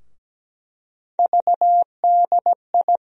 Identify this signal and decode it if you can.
Morse code, heard as a keyed tone, about 17 wpm, 700 Hz. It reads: VDI
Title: USA Call Sign Postfixes - 3 Letters 17wpm